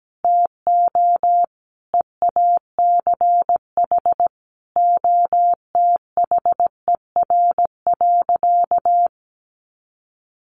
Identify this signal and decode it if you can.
Morse code, heard as a keyed tone, about 17 wpm, 700 Hz. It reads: TO EACH OTHER.
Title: Additional Easy Phrases 17wpm